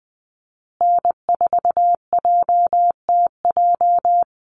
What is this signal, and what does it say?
Morse code, heard as a keyed tone, about 20 wpm, 700 Hz. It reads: N4JTJ
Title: USA Call Signs Repeated 3x 20wpm